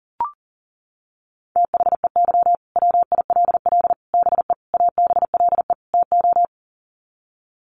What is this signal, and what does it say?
Morse code, heard as a keyed tone, about 40 wpm, 700 Hz. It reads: THEY WILL BE ABLE TO
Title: Sets of 5 Words Encore 40wpm